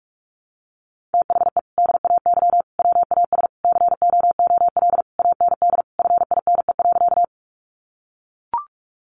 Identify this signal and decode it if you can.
Morse code, heard as a keyed tone, about 44 wpm, 700 Hz. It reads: THE DAY WAS COOL AND FINE.
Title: Sentences from Top 1200 Words 45wpm